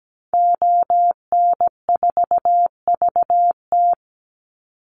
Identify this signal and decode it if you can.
Morse code, heard as a keyed tone, about 17 wpm, 700 Hz. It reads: ON4VT
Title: Call Signs - 17wpm